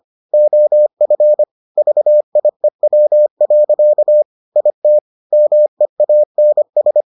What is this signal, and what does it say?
Morse code, heard as a keyed tone, about 25 wpm, 600 Hz. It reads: OF VIEW. IT MEANS